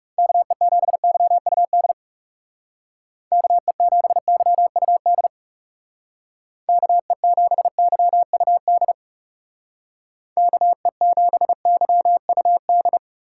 Morse code, heard as a keyed tone, about 36 wpm, 700 Hz. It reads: KE7YUD KE7YUD KE7YUD KE7YUD
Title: Speed Racing - USA 2x3 Call Signs 30wpm